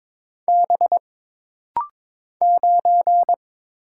Morse code, heard as a keyed tone, about 22 wpm, 700 Hz. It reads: B 9
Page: This rapid-fire timing is geared to push your ICR (Instant Character Recognition) skills.